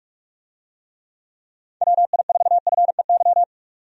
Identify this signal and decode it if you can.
Morse code, heard as a keyed tone, about 46 wpm, 700 Hz. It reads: WI4FEY
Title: USA Call Signs - 6 Characters - 45wpm